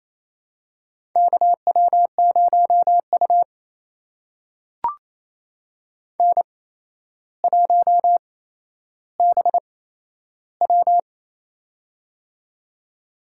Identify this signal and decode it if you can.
Morse code, heard as a keyed tone, about 28 wpm, spaced as for 12 wpm, 700 Hz. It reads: KW0U N1BW
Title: USA Call Signs - 4 Characters - 8x Character Spacing - 28wpm